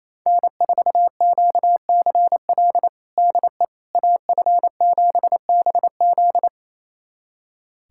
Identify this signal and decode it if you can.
Morse code, heard as a keyed tone, about 27 wpm, 700 Hz. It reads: N4QCL DE AF7BZ
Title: USA Callsign DE Callsign - 5 Characters - 28wpm